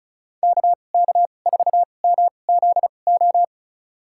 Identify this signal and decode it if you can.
Morse code, heard as a keyed tone, about 36 wpm, spaced as for 25 wpm, 700 Hz. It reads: KK4MZO
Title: USA Call Signs - 2x Character Spacing - 35wpm